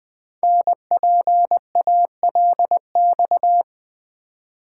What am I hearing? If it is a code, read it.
Morse code, heard as a keyed tone, about 20 wpm, 700 Hz. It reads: NPALX
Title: Code Groups - 5 Letter:Numbers - 20wpm